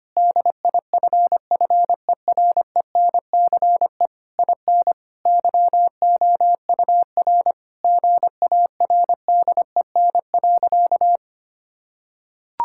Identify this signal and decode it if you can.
Morse code, heard as a keyed tone, about 25 wpm, 700 Hz. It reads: DIFFERENCE IN YOUR GARDEN.
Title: Sentences from Top 1000 Words - Review Part 5 - 25wpm